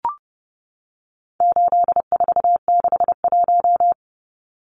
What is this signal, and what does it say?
Morse code, heard as a keyed tone, about 28 wpm, 700 Hz. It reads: 8461
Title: Numbers - 4-Digit 30wpm